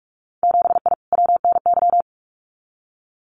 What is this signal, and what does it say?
Morse code, heard as a keyed tone, about 46 wpm, 700 Hz. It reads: 7I WNY
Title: Winter Field Day Exchanges Encore 45wpm